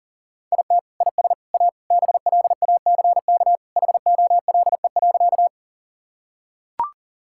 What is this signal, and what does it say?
Morse code, heard as a keyed tone, about 40 wpm, 700 Hz. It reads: IT IS A BLACK HOLE.
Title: Sentences from Top 1600 Words 40wpm